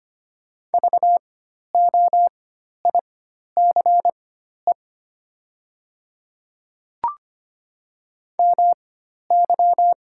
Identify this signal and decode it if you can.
Morse code, heard as a keyed tone, about 25 wpm, spaced as for 12 wpm, 700 Hz. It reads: VOICE MY
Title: 4x Character Spacing - Top 500 Words - 25wpm